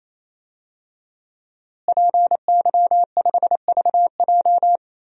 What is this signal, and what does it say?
Morse code, heard as a keyed tone, about 28 wpm, 700 Hz. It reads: PY5VJ
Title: DXCC-100 Call Signs - 5 Characters - Repeated 3x - 28wpm